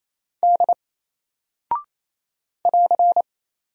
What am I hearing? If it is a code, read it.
Morse code, heard as a keyed tone, about 28 wpm, 700 Hz. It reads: D <AR>
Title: Single Character (Letter:Number:Punctuation:Prosign) Alt 28wpm